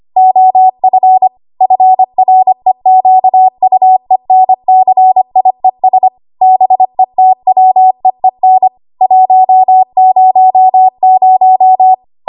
Morse code, heard as a keyed tone, about 25 wpm, 750 Hz. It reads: OF FREQUENCIES BETWEEN 100
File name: Archive / 25 WPM / 25WPM